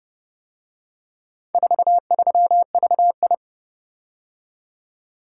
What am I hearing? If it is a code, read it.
Morse code, heard as a keyed tone, about 30 wpm, 700 Hz. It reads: 43VI